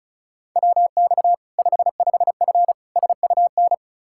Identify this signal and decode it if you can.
Morse code, heard as a keyed tone, about 36 wpm, 700 Hz. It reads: WX 55F SUN
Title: Practice QSO - Exchange 2 - Info - 35wpm